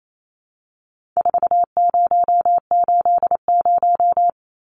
Morse code, heard as a keyed tone, about 28 wpm, 700 Hz. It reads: 4080
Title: Numbers - 4-Digit 28wpm